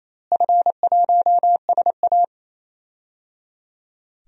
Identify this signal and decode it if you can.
Morse code, heard as a keyed tone, about 28 wpm, 700 Hz. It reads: F1SA
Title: DXCC-100 Call Signs - 4 Characters - Repeated 3x - 28wpm